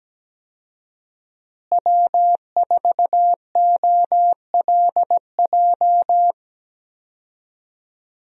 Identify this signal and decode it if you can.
Morse code, heard as a keyed tone, about 17 wpm, 700 Hz. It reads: W4OLJ
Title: USA Call Signs - 5 Characters - Repeated 2x - 17wpm